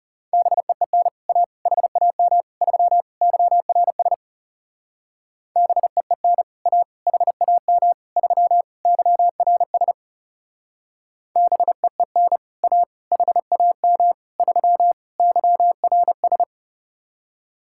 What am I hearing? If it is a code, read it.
Morse code, heard as a keyed tone, about 38 wpm, 700 Hz. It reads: BEEN A HAM 3 YRS BEEN A HAM 3 YRS BEEN A HAM 3 YRS